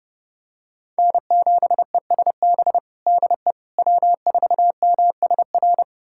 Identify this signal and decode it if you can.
Morse code, heard as a keyed tone, about 32 wpm, 700 Hz. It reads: N7ESB DE W4MSR
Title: USA Callsign DE Callsign - 5 Characters - 30wpm